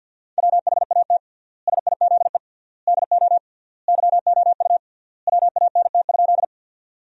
Morse code, heard as a keyed tone, about 51 wpm, 700 Hz. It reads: WHAT SIZE DO YOU WANT?
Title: 3x Word-Spacing - Sentences from Top 400 Words - 50wpm